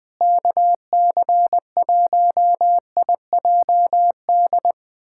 Morse code, heard as a keyed tone, about 21 wpm, 700 Hz. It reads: KC1IJD
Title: USA Callsign DE Callsign - 6 Characters - 20wpm